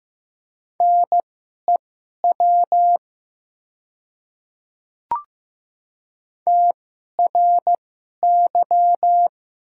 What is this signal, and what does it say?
Morse code, heard as a keyed tone, about 15 wpm, spaced as for 11 wpm, 700 Hz. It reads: NEW TRY
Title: 2x Character Spacing - Top 200 Words - 15wpm